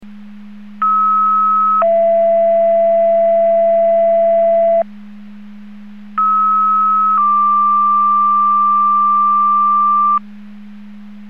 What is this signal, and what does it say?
Two-tone page, heard as a one-second tone, then a three-second tone, about 1290 then 670 Hz, about 1290 then 1180 Hz. It is heard from 0.8 to 10.2 s.